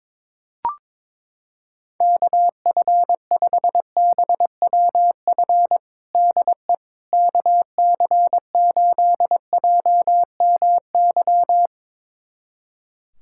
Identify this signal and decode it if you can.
Morse code, heard as a keyed tone, about 22 wpm, 700 Hz. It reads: KF5BWF DE KC8JMY
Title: USA Callsign DE Callsign - 6 Characters - 22wpm